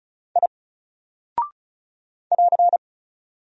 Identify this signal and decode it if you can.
Morse code, heard as a keyed tone, about 36 wpm, 700 Hz. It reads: I <AR>
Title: Single Character (Letter:Number:Punctuation:Prosign) Alt 35wpm